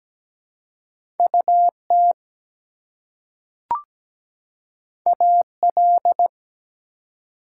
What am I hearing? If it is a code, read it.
Morse code, heard as a keyed tone, about 17 wpm, 700 Hz. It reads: UT AL
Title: US States and Canadian Provinces 17wpm